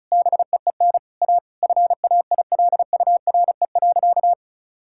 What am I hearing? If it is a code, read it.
Morse code, heard as a keyed tone, about 36 wpm, 700 Hz. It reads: BEEN A FAILURE.
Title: Sentences from Top 1000 Words - Review Part 6 - 35wpm